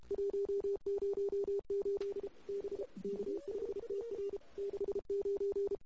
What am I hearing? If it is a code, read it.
Morse code, heard as a keyed tone, about 32 wpm, 410 Hz. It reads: 108 B K5TR 69
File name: CW / 2008-ss-cw-k5tr-op-n5rz -21:00